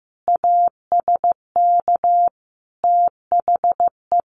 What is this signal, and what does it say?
Morse code, heard as a keyed tone, about 14 wpm, 700 Hz. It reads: ASK THE
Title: Sentences from Top 600 Words 15wpm